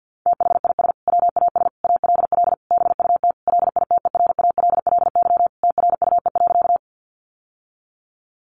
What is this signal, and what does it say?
Morse code, heard as a keyed tone, about 50 wpm, 700 Hz. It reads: THIS WAS ALL BUT LITERALLY TRUE.
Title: Sentences from Top 1400 Words 50wpm